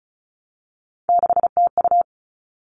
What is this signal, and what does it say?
Morse code, heard as a keyed tone, about 34 wpm, 700 Hz. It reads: 6TU
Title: All Cut Numbers - 3-Digit 35wpm